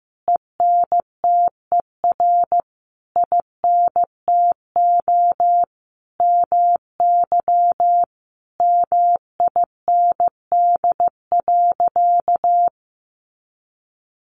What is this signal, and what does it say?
Morse code, heard as a keyed tone, about 15 wpm, 700 Hz. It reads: ENTER INTO MY MIND.
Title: Sentences from Top 1300 Words 15wpm